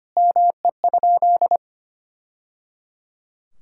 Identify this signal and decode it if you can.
Morse code, heard as a keyed tone, about 25 wpm, 700 Hz. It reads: ME?
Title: Sentences from Top 1000 Words - Review Part 5 - 25wpm